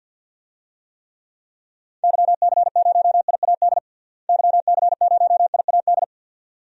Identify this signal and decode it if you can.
Morse code, heard as a keyed tone, about 50 wpm, 700 Hz. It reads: YC0IAD YC0IAD
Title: DXCC-100 Call Signs - 6 Characters - Repeated 2x - 50wpm